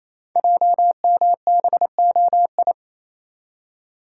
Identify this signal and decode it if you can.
Morse code, heard as a keyed tone, about 27 wpm, 700 Hz. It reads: JMBOI
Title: Code Groups - 5 Letters - 28wpm